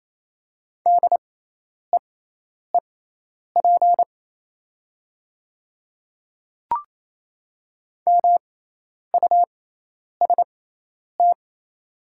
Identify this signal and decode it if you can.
Morse code, heard as a keyed tone, about 28 wpm, spaced as for 10 wpm, 690 Hz. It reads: DEEP MUST
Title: 6x Character Spacing - Top 500 Words - 28wpm